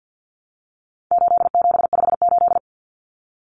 Morse code, heard as a keyed tone, about 50 wpm, 700 Hz. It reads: 8758
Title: Numbers - 4-Digit 50wpm